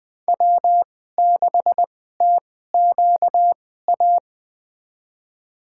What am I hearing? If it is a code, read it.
Morse code, heard as a keyed tone, about 20 wpm, spaced as for 14 wpm, 700 Hz. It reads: W6TQA
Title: USA Call Signs - 5 Characters - 2x Character Spacing - 20wpm